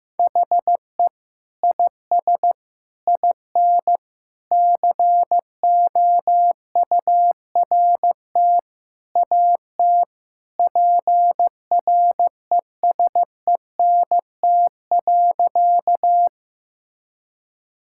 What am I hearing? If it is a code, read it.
Morse code, heard as a keyed tone, about 15 wpm, 700 Hz. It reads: HE IS IN COURT AT PRESENT.
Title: Sentences from Top 1600 Words 15wpm